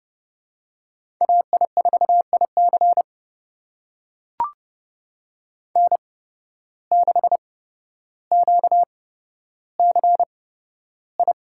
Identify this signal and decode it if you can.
Morse code, heard as a keyed tone, about 32 wpm, spaced as for 13 wpm, 700 Hz. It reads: AI4IC N6QCI
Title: USA Call Signs - 5 Characters - 8x Character Spacing - 30wpm